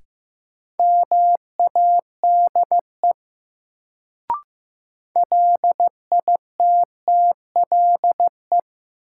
Morse code, heard as a keyed tone, about 16 wpm, 700 Hz. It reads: MADE LITTLE
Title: Sentences from Top 200 Words - New Word Drill - 15wpm